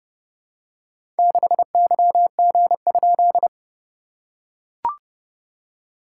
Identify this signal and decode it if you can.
Morse code, heard as a keyed tone, about 30 wpm, 690 Hz. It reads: 6YG?
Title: SOTA QSO Practice 30wpm